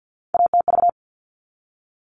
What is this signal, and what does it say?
Morse code, heard as a keyed tone, about 50 wpm, 700 Hz. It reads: ATV
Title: All Cut Numbers - 3-Digit 50wpm